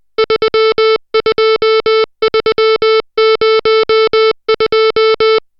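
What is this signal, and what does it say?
Morse code, heard as a keyed tone, about 20 wpm, 440 Hz, with noise. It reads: 32302
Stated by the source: Morse code drill of digits at 60 characters per minute.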